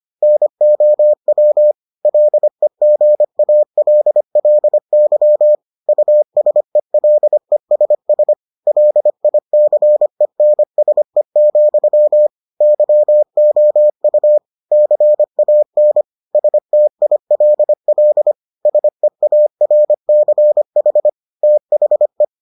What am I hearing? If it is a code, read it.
Morse code, heard as a keyed tone, about 25 wpm, 600 Hz. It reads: NOW LEGALLY USELESS LICENSE, YOU CAN STILL SEARCH THE